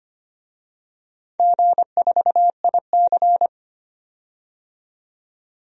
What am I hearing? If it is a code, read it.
Morse code, heard as a keyed tone, about 25 wpm, 690 Hz. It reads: G4IC
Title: DXCC-100 Call Signs - 4 Characters - 25wpm